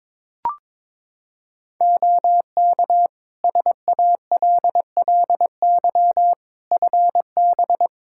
Morse code, heard as a keyed tone, about 22 wpm, 700 Hz. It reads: OK SALLY FB
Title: Practice QSO - Exchange 2 - Info - 22wpm